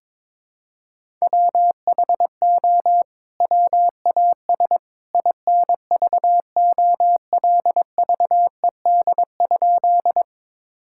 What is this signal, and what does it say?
Morse code, heard as a keyed tone, about 22 wpm, 700 Hz. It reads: WHO WAS INVOLVED?
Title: Sentences from Top 1000 Words - Review Part 5 - 22wpm